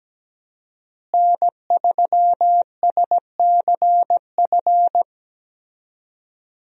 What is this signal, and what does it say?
Morse code, heard as a keyed tone, about 17 wpm, 700 Hz. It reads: N3SCF
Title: USA Call Signs - 5 Characters - Repeated 2x - 17wpm